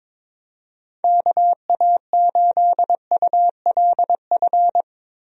Morse code, heard as a keyed tone, about 22 wpm, 700 Hz. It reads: KA8ULF
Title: USA Call Signs - 6 Characters - 22wpm